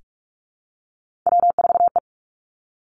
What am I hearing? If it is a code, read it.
Morse code, heard as a keyed tone, about 46 wpm, 700 Hz. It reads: W4E
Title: Three Challenging Letter-Number Combinations 45wpm